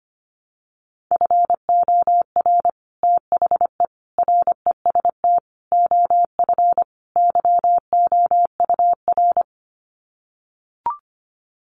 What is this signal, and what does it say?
Morse code, heard as a keyed tone, about 25 wpm, 700 Hz. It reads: FOR THE REST OF YOUR
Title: Sets of 5 Words Encore 25wpm